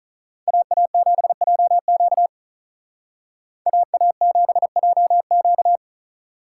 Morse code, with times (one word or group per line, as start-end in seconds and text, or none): 0.5-2.3 AA7JQ
3.7-5.8 AA7JQ